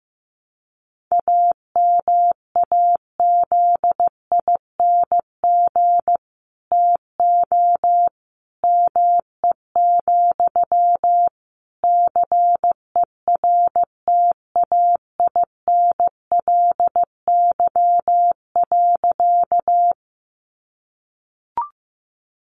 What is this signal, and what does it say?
Morse code, heard as a keyed tone, about 15 wpm, 700 Hz. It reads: AMAZING TO ME, CERTAINLY.
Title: Sentences from Top 1500 Words 15wpm